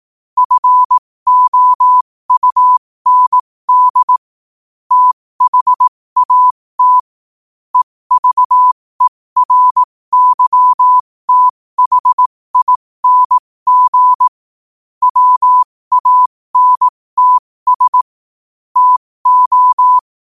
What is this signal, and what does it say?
Morse code, heard as a keyed tone, about 18 wpm, spaced as for 15 wpm, 1000 Hz. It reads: FOUND THAT EVERYTHING WANTS TO
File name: qotdfiles / 15wpm